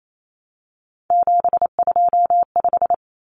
Morse code, heard as a keyed tone, about 28 wpm, 700 Hz. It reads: 725
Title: Common Cut Numbers - 3-Digit 28wpm